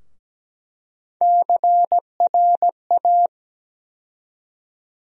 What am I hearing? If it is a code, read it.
Morse code, heard as a keyed tone, about 17 wpm, 700 Hz. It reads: CRA
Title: USA Call Sign Postfixes - 3 Letters 17wpm